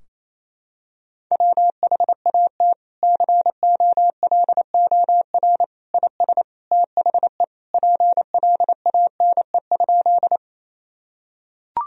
Morse code, heard as a keyed tone, about 27 wpm, 700 Hz. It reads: WHAT COLOR IS THE PLANE?
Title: Sentences from Top 1000 Words - Review Part 3 - 28wpm